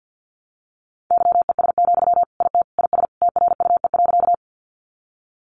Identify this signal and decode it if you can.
Morse code, heard as a keyed tone, about 50 wpm, 700 Hz. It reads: YES, IT IS TRUE.
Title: Sentences from Top 500 Words 50wpm